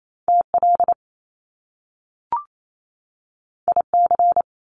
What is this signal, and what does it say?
Morse code, heard as a keyed tone, about 28 wpm, 700 Hz. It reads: TL IC
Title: Top Two Letter Combinations in Words 28wpm